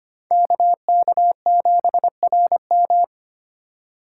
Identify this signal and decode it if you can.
Morse code, heard as a keyed tone, about 25 wpm, 700 Hz. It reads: KK7RM
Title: USA Call Signs - 5 Characters - 3x Character Spacing - 25wpm